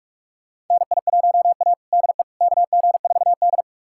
Morse code, heard as a keyed tone, about 46 wpm, 700 Hz. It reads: NI1A DE KG4D
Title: USA Callsign DE Callsign - 4 Characters - 45wpm